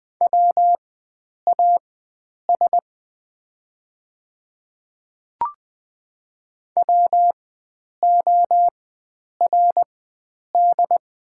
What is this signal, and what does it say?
Morse code, heard as a keyed tone, about 20 wpm, spaced as for 9 wpm, 690 Hz. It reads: WAS WORD
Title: 4x Character Spacing - Top 100 Words - 20wpm